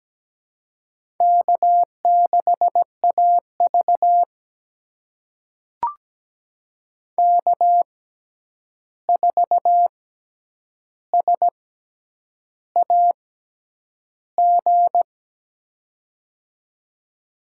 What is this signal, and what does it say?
Morse code, heard as a keyed tone, about 17 wpm, spaced as for 8 wpm, 700 Hz. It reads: K6AV K4SAG
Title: USA Call Signs - 6x Character Spacing - 17wpm